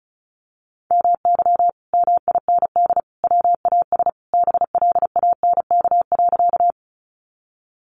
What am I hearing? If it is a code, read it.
Morse code, heard as a keyed tone, about 34 wpm, 700 Hz. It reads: MY MIND WAS BLANK.
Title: Sentences from Top 1500 Words 35wpm